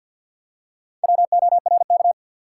Morse code, heard as a keyed tone, about 47 wpm, 700 Hz. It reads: WORK
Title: Four Letter Words - 50wpm